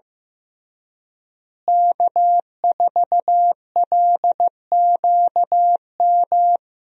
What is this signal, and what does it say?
Morse code, heard as a keyed tone, about 15 wpm, 700 Hz. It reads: K4LQM
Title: USA Call Signs - 5 Characters - Repeated 3x - 15wpm